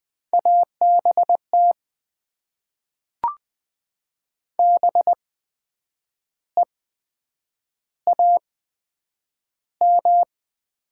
Morse code, heard as a keyed tone, about 20 wpm, spaced as for 8 wpm, 700 Hz. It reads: ABT BEAM
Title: Top 90 QSO Elements - 8x Character Spacing - 20wpm